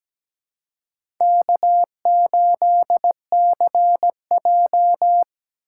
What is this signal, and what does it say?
Morse code, heard as a keyed tone, about 20 wpm, 700 Hz. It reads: K8CJ
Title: Speed Racing - USA Call Signs 17wpm